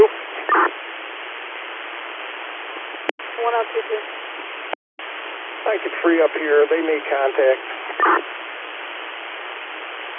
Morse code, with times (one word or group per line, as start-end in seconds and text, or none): none